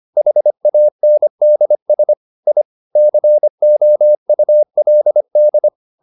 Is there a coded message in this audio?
Yes: Morse code from 0.2 to 5.7 s, heard as a keyed tone, about 25 wpm, 600 Hz. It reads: HANDS I COULD